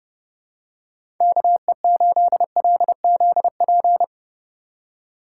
Morse code, heard as a keyed tone, about 30 wpm, 700 Hz. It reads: KE8LZP
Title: USA Call Signs - 6 Characters - 3x Character Spacing - 30wpm